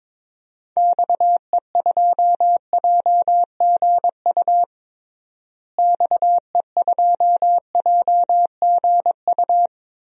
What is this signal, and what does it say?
Morse code, heard as a keyed tone, about 22 wpm, 700 Hz. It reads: XE2JGU XE2JGU
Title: DXCC-100 Call Signs - 6 Characters - Repeated 2x - 22wpm